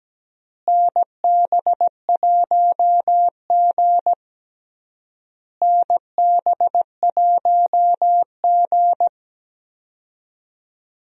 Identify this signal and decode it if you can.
Morse code, heard as a keyed tone, about 17 wpm, 700 Hz. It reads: NB1G NB1G
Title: USA Call Signs - 4 Characters - Repeated 3x - 17wpm